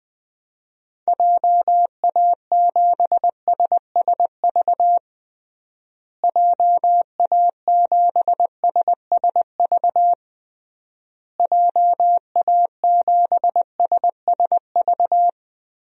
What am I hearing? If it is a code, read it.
Morse code, heard as a keyed tone, about 21 wpm, 700 Hz. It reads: JA7SSV JA7SSV JA7SSV
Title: DXCC-100 Call Signs - 6 Characters - Repeated 3x - 20wpm